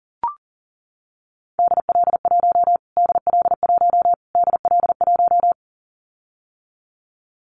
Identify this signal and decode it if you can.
Morse code, heard as a keyed tone, about 40 wpm, 700 Hz. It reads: DL1 DL1 DL1
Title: Callsign Prefixes for Top 25 DXCC - Repeated 3x - 40wpm